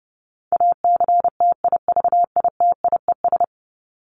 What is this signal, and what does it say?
Morse code, heard as a keyed tone, about 30 wpm, 700 Hz. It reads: ACTIVITIES
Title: ES:S Suffix - Top 100 Words with Base Words 30wpm